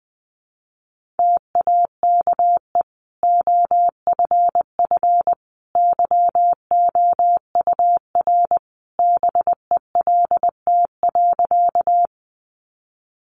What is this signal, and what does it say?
Morse code, heard as a keyed tone, about 20 wpm, 700 Hz. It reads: TAKE OFF YOUR BELT.
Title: Sentences from Top 1400 Words 20wpm